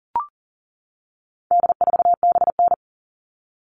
Morse code, heard as a keyed tone, about 40 wpm, 700 Hz. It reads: D4BN
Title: All Cut Numbers - 4-Digit 40wpm